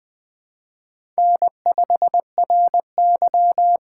Morse code, heard as a keyed tone, about 20 wpm, 700 Hz. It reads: N5RY
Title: USA Call Signs - 6x Character Spacing - 20wpm